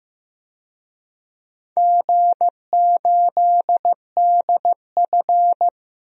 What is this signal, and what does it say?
Morse code, heard as a keyed tone, about 15 wpm, 700 Hz. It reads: G8DF
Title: DXCC-100 Call Signs - 4 Characters - Repeated 3x - 15wpm